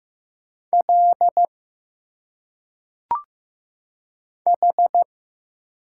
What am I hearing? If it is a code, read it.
Morse code, heard as a keyed tone, about 15 wpm, 700 Hz. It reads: L H
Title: Single Letters - Phonetic Pronunciation 15wpm